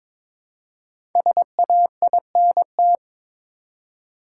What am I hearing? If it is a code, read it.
Morse code, heard as a keyed tone, about 22 wpm, 700 Hz. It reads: SAINT
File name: 5 Letter Words - First 3 Letters Given - 22wpm